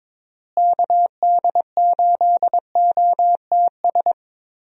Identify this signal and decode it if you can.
Morse code, heard as a keyed tone, about 22 wpm, 700 Hz. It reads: KD8OTS
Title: USA Call Signs - 6 Characters - Repeated 3x - 22wpm